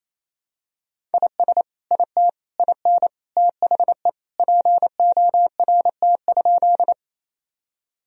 Morse code, heard as a keyed tone, about 28 wpm, 700 Hz. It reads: IS IT IN THE PORT?
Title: Sentences from Top 1000 Words - Review Part 1 - 28wpm